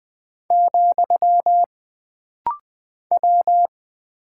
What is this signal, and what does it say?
Morse code, heard as a keyed tone, about 20 wpm, 700 Hz. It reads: , W
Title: Single Character (Letter:Number:Punctuation:Prosign) - Rapid-Fire - 20wpm